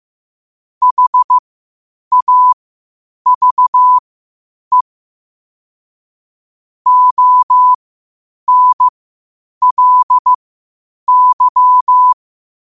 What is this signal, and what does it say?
Morse code, heard as a keyed tone, about 15 wpm, spaced as for 8 wpm, 1000 Hz. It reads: HAVE ONLY
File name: qotdfiles / 8wpm